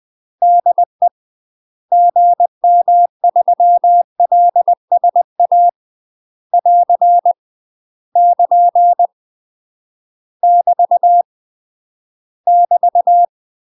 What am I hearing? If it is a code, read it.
Morse code, heard as a keyed tone, about 20 wpm, 700 Hz. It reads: DE GM3LSA <AR> <KN> <BT> <BT>
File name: UK MASTER QSOs TRAINING / 20WPM / UK MASTER QSOs 0027